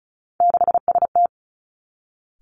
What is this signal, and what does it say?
Morse code, heard as a keyed tone, about 36 wpm, 700 Hz. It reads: 6ST
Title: Three Challenging Letter-Number Combinations 35wpm